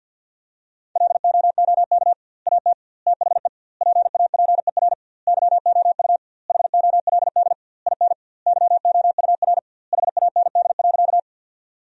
Morse code, heard as a keyed tone, about 50 wpm, 700 Hz. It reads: LOOK AT THE PAPER YOU HOLD IN YOUR HAND.
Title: 2x Word-Spacing - Sentences from Top 400 Words - 50wpm